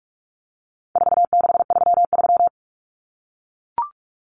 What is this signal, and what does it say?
Morse code, heard as a keyed tone, about 44 wpm, 700 Hz. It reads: 4633
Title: Common Cut Numbers - 4-Digit 45wpm